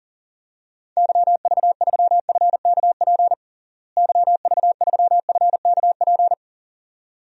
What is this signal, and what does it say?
Morse code, heard as a keyed tone, about 37 wpm, 700 Hz. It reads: YV3FKP YV3FKP
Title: DXCC-100 Call Signs - 6 Characters - Repeated 2x - 40wpm